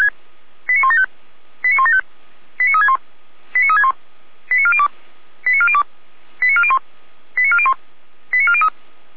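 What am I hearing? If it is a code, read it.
Five-tone selective call (ZVEI), heard as quick runs of five tones, tones of about 70 ms. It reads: 89175 89175 89361 89361 89402 89402 89401 89401 89403
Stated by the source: ZVEI five-tone call 89192 received on SDR.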